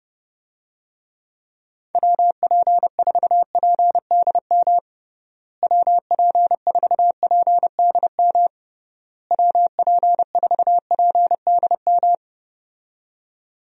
Morse code, heard as a keyed tone, about 30 wpm, 700 Hz. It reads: WP4PDM WP4PDM WP4PDM
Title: USA Call Signs - 6 Characters - Repeated 3x - 30wpm